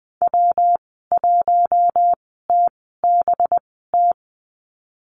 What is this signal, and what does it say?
Morse code, heard as a keyed tone, about 20 wpm, spaced as for 15 wpm, 700 Hz. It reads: W1TBT
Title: 186 Lesson 30g - Letter b - Call Signs 20wpm